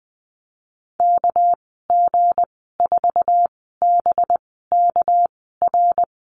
Morse code, heard as a keyed tone, about 20 wpm, spaced as for 14 wpm, 700 Hz. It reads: KG4BKR
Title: 238 Lesson 38g - Letter q - Call Signs 20wpm